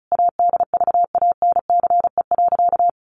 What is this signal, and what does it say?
Morse code, heard as a keyed tone, about 34 wpm, 700 Hz. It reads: ADVANCE.
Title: Sentences from Top 1400 Words 35wpm